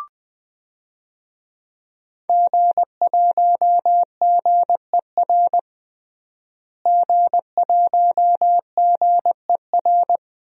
Morse code, heard as a keyed tone, about 20 wpm, 700 Hz. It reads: G1GER G1GER
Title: DXCC-100 Call Signs - 5 Characters - Repeated 3x - 20wpm